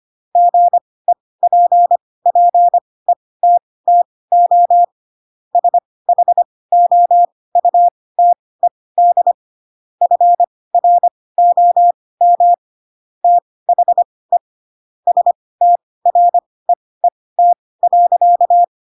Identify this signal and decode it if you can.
Morse code, heard as a keyed tone, about 25 wpm, spaced as for 18 wpm, 700 Hz. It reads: GEPPETTO SHOUTED FROM THE STREET.
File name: Pinocchio / Pinocchio ch7 12 25 ik1hge